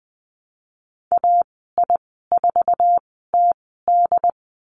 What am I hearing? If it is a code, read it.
Morse code, heard as a keyed tone, about 20 wpm, spaced as for 15 wpm, 700 Hz. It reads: AI4TD
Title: 193 Lesson 31g - Letter v - Call Signs 20wpm